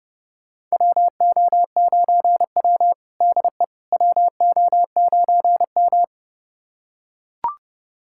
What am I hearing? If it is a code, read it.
Morse code, heard as a keyed tone, about 30 wpm, 700 Hz. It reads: WO9W DE WO9M
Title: USA Callsign DE Callsign - 4 Characters - 30wpm